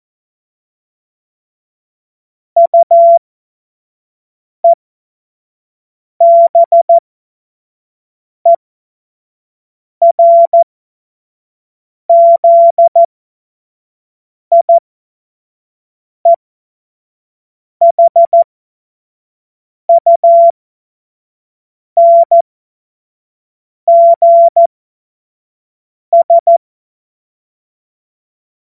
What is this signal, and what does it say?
Morse code, heard as a keyed tone, about 14 wpm, spaced as for 5 wpm, 680 Hz. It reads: UEBERZIEHUNGS
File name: Morsekurs / Witze / Witz 02. Mai 2023 25 F70